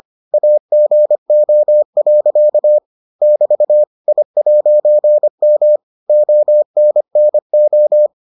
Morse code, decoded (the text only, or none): AGO. <BT> I'M ONNO